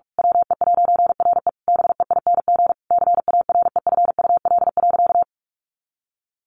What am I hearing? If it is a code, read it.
Morse code, heard as a keyed tone, about 44 wpm, 700 Hz. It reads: WE'RE BEING CAREFUL.
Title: A Sound Of Thunder 45wpm